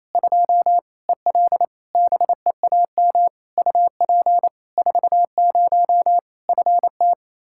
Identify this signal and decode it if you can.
Morse code, heard as a keyed tone, about 28 wpm, 700 Hz. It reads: 2 EL BEAM UP 40 FT
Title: Practice QSO - Exchange 2 - Info - 28wpm